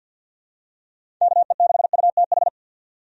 Morse code, heard as a keyed tone, about 51 wpm, 700 Hz. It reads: KE6UTH
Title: USA Call Signs - 6 Characters - 50wpm